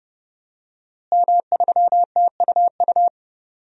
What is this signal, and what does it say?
Morse code, heard as a keyed tone, about 30 wpm, 700 Hz. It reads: M3TUU
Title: DXCC-100 Call Signs - 5 Characters - Repeated 2x - 30wpm